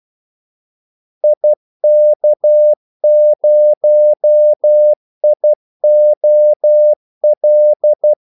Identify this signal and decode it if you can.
Morse code, heard as a keyed tone, about 12 wpm, 600 Hz. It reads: IK0IOL
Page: 2 ekstra ordmellemrum mellem kaldesignalerne